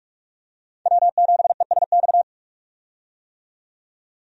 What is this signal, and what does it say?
Morse code, heard as a keyed tone, about 46 wpm, 700 Hz. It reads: W7ESX
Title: USA Call Signs - 5 Characters - 45wpm